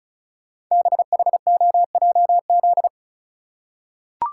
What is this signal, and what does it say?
Morse code, heard as a keyed tone, about 36 wpm, 700 Hz. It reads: BHOJZ
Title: Code Groups - 5 Letters - 35wpm